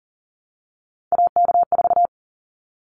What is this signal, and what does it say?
Morse code, heard as a keyed tone, about 40 wpm, 700 Hz. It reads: AK4
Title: Callsign Prefixes for Top 25 DXCC - 40wpm